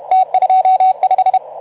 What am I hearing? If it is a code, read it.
Morse code, heard as a keyed tone, about 34 wpm, 750 Hz, with radio noise. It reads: T25
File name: Archiv / CUC181 / OK2ZC CUC 181